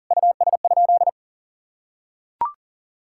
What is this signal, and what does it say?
Morse code, heard as a keyed tone, about 40 wpm, 700 Hz. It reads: US?
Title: Sentences from Top 1000 Words - Review Part 5 - 40wpm